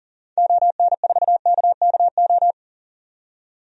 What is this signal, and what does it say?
Morse code, heard as a keyed tone, about 40 wpm, 700 Hz. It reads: ON4KKO
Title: DXCC-100 Call Signs - 6 Characters - 40wpm